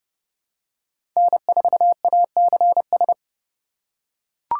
Morse code, heard as a keyed tone, about 30 wpm, 700 Hz. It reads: N4ACS
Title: USA Call Signs - 5 Characters - 4x Character Spacing - 30wpm